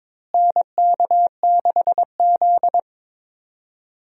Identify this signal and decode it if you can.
Morse code, heard as a keyed tone, about 23 wpm, 700 Hz. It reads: NK6Z
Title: USA Call Signs - 4 Characters - Repeated 3x - 22wpm